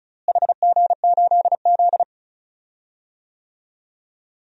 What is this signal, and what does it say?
Morse code, heard as a keyed tone, about 34 wpm, 700 Hz. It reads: HG8Z
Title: DXCC-100 Call Signs - 4 Characters - Repeated 3x - 35wpm